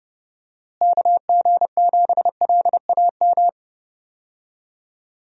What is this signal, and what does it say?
Morse code, heard as a keyed tone, about 30 wpm, 700 Hz. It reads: KG7LAM
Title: USA Call Signs - 6 Characters - Repeated 3x - 30wpm